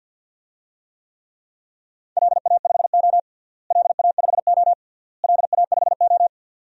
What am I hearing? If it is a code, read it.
Morse code, heard as a keyed tone, about 52 wpm, 700 Hz. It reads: LA5O LA5O LA5O
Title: DXCC-100 Call Signs - 4 Characters - Repeated 3x - 50wpm